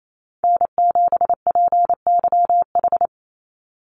Morse code, heard as a keyed tone, about 28 wpm, 700 Hz. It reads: N7PYH
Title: USA Call Signs 28wpm